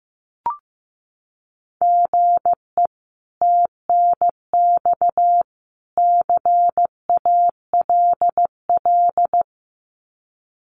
Morse code, heard as a keyed tone, about 14 wpm, 700 Hz. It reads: GE TNX CALL
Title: Common QSO Phrases 15wpm